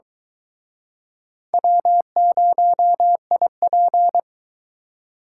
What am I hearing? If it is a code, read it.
Morse code, heard as a keyed tone, about 22 wpm, 700 Hz. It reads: W0IP